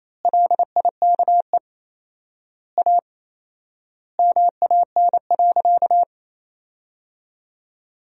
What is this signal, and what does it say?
Morse code, heard as a keyed tone, about 28 wpm, 700 Hz. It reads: LIKE A MAN.
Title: 4x Word-Spacing - Sentences from Top 300 Words - 28wpm